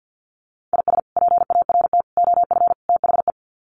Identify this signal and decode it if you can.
Morse code, heard as a keyed tone, about 50 wpm, 700 Hz. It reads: IS PART OF THE
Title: Sets of 4 Words 50wpm